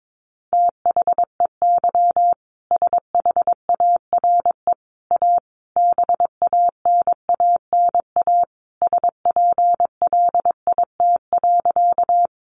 Morse code, heard as a keyed tone, about 22 wpm, 700 Hz. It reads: THEY SHARE A BANANA SPLIT.
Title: Easy Phrases 22wpm